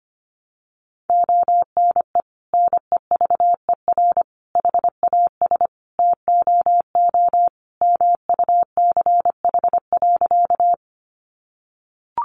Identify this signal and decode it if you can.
Morse code, heard as a keyed tone, about 25 wpm, 700 Hz. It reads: ONE NEVER HAS TOO MUCH.
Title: Sentences from Top 300 Words 25wpm